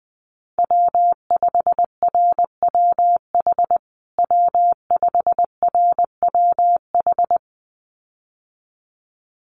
Morse code, heard as a keyed tone, about 20 wpm, 700 Hz. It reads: W5RWH W5RWH
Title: USA Call Signs Repeated 2x 20wpm